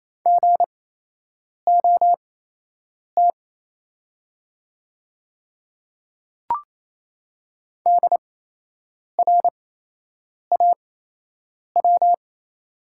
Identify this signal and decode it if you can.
Morse code, heard as a keyed tone, about 28 wpm, spaced as for 8 wpm, 700 Hz. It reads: GOT DRAW
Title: 8x Character Spacing - Top 300 Words - 28wpm